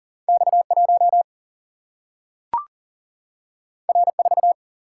Morse code, heard as a keyed tone, about 40 wpm, 700 Hz. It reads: X1 R4
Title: Single-Letter Single-Number 40wpm